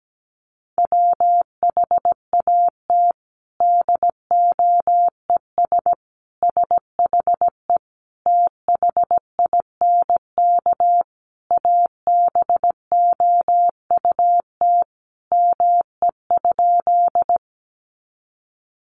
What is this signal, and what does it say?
Morse code, heard as a keyed tone, about 17 wpm, 700 Hz. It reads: WHAT DOES SHE THINK ABOUT ME?
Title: Sentences from Top 200 Words 17wpm